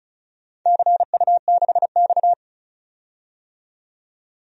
Morse code, heard as a keyed tone, about 36 wpm, 690 Hz. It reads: CU6X
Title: DXCC-100 Call Signs - 4 Characters - 35wpm